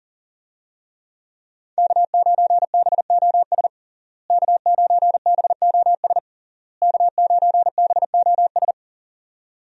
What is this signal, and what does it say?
Morse code, heard as a keyed tone, about 40 wpm, 690 Hz. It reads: K9BOS K9BOS K9BOS
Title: USA Call Signs - 5 Characters - Repeated 3x - 40wpm